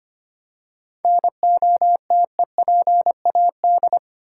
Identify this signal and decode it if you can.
Morse code, heard as a keyed tone, about 25 wpm, 690 Hz. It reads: NOTEPAD
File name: practice / 25wpm / Easy Words with Hints - 25wpm